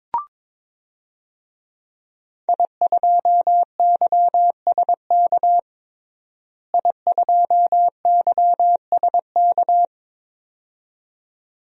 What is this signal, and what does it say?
Morse code, heard as a keyed tone, about 22 wpm, 700 Hz. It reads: I2YSK I2YSK
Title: DXCC-100 Call Signs - 5 Characters - Repeated 2x - 22wpm